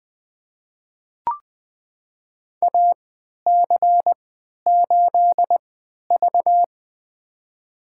Morse code, heard as a keyed tone, about 20 wpm, spaced as for 11 wpm, 700 Hz. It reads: AC8V
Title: USA Call Signs - 4 Characters - 3x Character Spacing - 20wpm